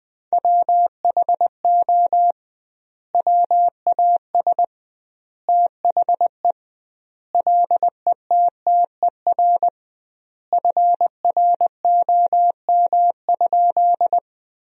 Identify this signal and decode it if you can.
Morse code, heard as a keyed tone, about 21 wpm, 700 Hz. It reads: WHO WAS THE LETTER FROM?
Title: 2x Word-Spacing - Sentences from Top 300 Words - 20wpm